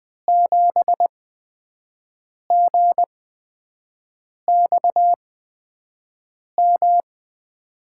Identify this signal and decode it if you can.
Morse code, heard as a keyed tone, about 20 wpm, spaced as for 5 wpm, 700 Hz. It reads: 7GXM
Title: USA Call Signs - 6 Characters - 8x Character Spacing - 20wpm